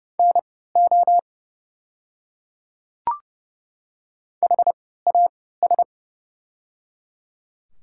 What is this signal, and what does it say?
Morse code, heard as a keyed tone, about 29 wpm, spaced as for 17 wpm, 690 Hz. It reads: NO HAS
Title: 3x Character Spacing - Top 300 Words - 30wpm